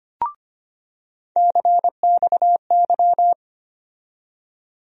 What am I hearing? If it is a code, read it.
Morse code, heard as a keyed tone, about 25 wpm, 700 Hz. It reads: CXY
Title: USA Call Sign Postfixes - 3 Letters 25wpm